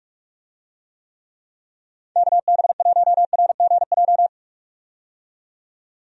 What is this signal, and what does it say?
Morse code, heard as a keyed tone, about 46 wpm, 700 Hz. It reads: KB1RGJ
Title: USA Call Signs - 6 Characters - 45wpm